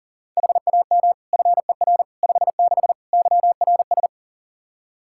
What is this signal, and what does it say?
Morse code, heard as a keyed tone, about 40 wpm, 700 Hz. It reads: HAM FER 56 YRS
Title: Practice QSO - Exchange 3 - More Info - 40wpm